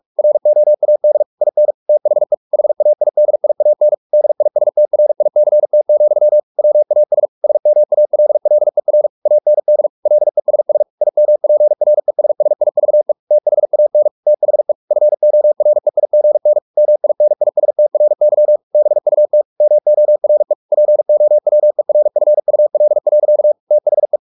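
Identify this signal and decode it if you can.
Morse code, heard as a keyed tone, about 44 wpm, 600 Hz. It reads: ROAD IN THE HAIDIAN DISTRICT, WAS SMALLER AND LESS IMPRESSIVE THAN THE FOREIGN MINISTRY BUT MORE POWERFUL. THE
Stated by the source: Morse practice at 45 wpm, the novel Never by Ken Follett in English, from CW med Gnister.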